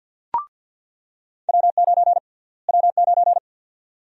Morse code, heard as a keyed tone, about 50 wpm, 700 Hz. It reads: W9 W9
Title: USA Call Sign Prefixes - Repeated 2x - 50wpm